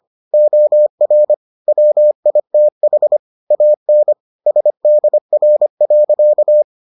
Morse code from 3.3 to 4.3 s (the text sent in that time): AN